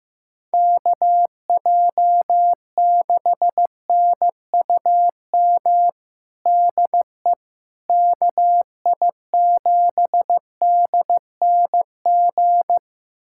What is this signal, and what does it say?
Morse code, heard as a keyed tone, about 15 wpm, 700 Hz. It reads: KJ6NUM DE KI7DNG
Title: USA Callsign DE Callsign - 6 Characters - 15wpm